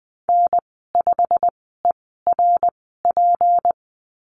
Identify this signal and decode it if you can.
Morse code, heard as a keyed tone, about 20 wpm, spaced as for 15 wpm, 700 Hz. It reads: N5ERP
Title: 219 Lesson 35g - Number 0 - Call Signs 20wpm